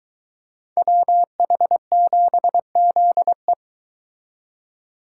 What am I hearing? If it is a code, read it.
Morse code, heard as a keyed tone, about 23 wpm, 700 Hz. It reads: WH7ZE